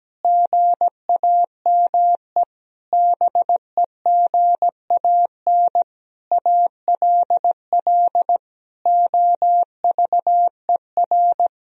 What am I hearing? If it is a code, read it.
Morse code, heard as a keyed tone, about 17 wpm, 700 Hz. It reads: GAME BEGAN ALL OVER
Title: Sentences from Top 500 Words and Abbreviations 17wpm